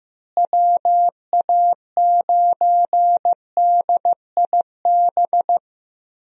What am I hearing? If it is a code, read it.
Morse code, heard as a keyed tone, about 15 wpm, 700 Hz. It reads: WA9DIB
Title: USA Call Signs - 6 Characters - Repeated 2x - 15wpm